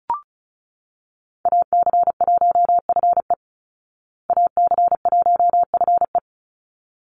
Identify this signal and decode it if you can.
Morse code, heard as a keyed tone, about 34 wpm, 700 Hz. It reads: AC1FE AC1FE
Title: Call Signs Repeated 3x - 35wpm